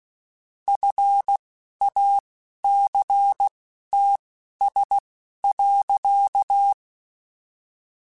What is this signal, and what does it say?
Morse code, heard as a keyed tone, about 16 wpm, spaced as for 12 wpm, 800 Hz. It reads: FACTS.
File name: Quotes / Quotes-mp3s / IgnoringFacts @10